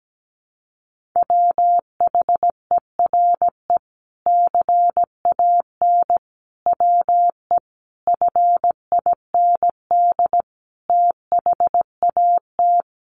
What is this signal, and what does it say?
Morse code, heard as a keyed tone, about 17 wpm, 700 Hz. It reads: WHERE CAN WE FIND THAT
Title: Sentences from Top 1500 Words 17wpm